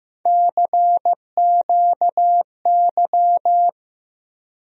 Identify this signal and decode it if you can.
Morse code, heard as a keyed tone, about 15 wpm, 700 Hz. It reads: CQY
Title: USA Call Sign Postfixes - 3 Letters 15wpm